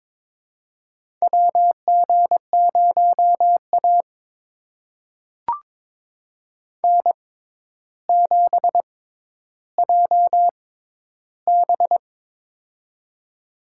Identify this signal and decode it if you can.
Morse code, heard as a keyed tone, about 21 wpm, spaced as for 11 wpm, 700 Hz. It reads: WG0A N7JB
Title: USA Call Signs - 4 Characters - 6x Character Spacing - 22wpm